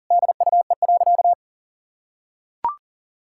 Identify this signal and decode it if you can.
Morse code, heard as a keyed tone, about 40 wpm, 700 Hz. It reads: DUE.
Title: Sentences from Top 1000 Words - Review Part 6 - 40wpm